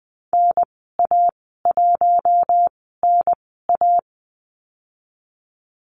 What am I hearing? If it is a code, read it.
Morse code, heard as a keyed tone, about 20 wpm, spaced as for 15 wpm, 700 Hz. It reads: NA1NA
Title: 042 Lesson 8g - Number 1 - Call Signs 20wpm